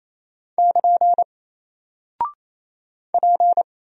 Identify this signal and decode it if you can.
Morse code, heard as a keyed tone, about 27 wpm, 700 Hz. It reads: <KN> P
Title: Single Character (Letter:Number:Punctuation:Prosign) 28wpm